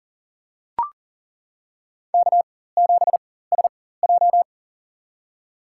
Morse code, heard as a keyed tone, about 37 wpm, spaced as for 18 wpm, 700 Hz. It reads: K7SJ
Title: USA Call Signs - 4 Characters - 4x Character Spacing - 40wpm